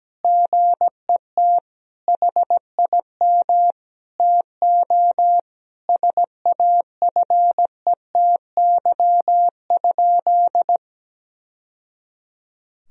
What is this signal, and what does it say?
Morse code, heard as a keyed tone, about 17 wpm, 700 Hz. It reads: GET HIM TO SAFETY?
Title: Sentences from Top 1000 Words - Review Part 4 - 17wpm